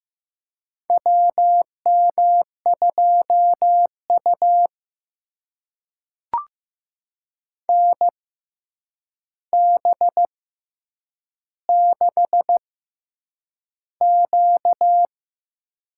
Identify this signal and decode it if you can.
Morse code, heard as a keyed tone, about 15 wpm, spaced as for 8 wpm, 700 Hz. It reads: WM2U NB6Q
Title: USA Call Signs - 4 Characters - 6x Character Spacing - 15wpm